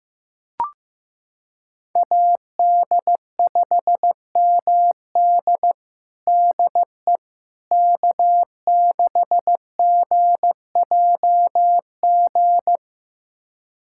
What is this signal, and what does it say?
Morse code, heard as a keyed tone, about 16 wpm, 700 Hz. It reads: AD5MD DE K6GJG
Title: USA Callsign DE Callsign - 5 Characters - 15wpm